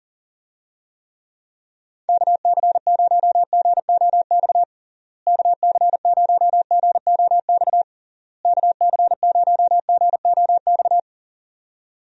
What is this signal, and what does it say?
Morse code, heard as a keyed tone, about 37 wpm, 700 Hz. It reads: KC0GOX KC0GOX KC0GOX
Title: USA Call Signs - 6 Characters - Repeated 3x - 40wpm